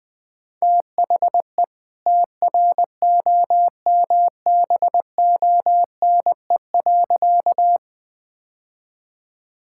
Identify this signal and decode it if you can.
Morse code, heard as a keyed tone, about 20 wpm, 700 Hz. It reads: THE TROMBONE.
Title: All-Age Appropriate Jokes - 20wpm